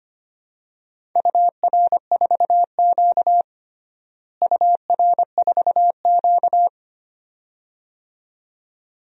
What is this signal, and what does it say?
Morse code, heard as a keyed tone, about 25 wpm, 700 Hz. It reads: UR4Q UR4Q
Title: DXCC-100 Call Signs - 4 Characters - Repeated 2x - 25wpm